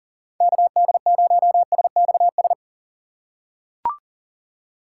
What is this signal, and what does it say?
Morse code, heard as a keyed tone, about 38 wpm, 700 Hz. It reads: KD0SXS
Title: USA Call Signs - 6 Characters - 6x Character Spacing - 40wpm